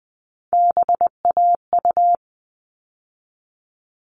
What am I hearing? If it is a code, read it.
Morse code, heard as a keyed tone, about 20 wpm, 700 Hz. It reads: BAU
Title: All Cut Numbers - 3-Digit 20wpm